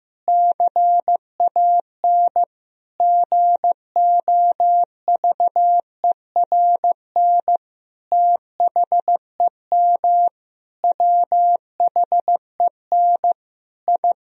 Morse code, heard as a keyed tone, about 15 wpm, 700 Hz. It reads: CAN GOVERN THEM WHEN I
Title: Sentences from Top 1000 Words - Review Part 2 - 15wpm